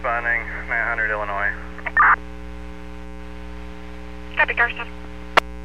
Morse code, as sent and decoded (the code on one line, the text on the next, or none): none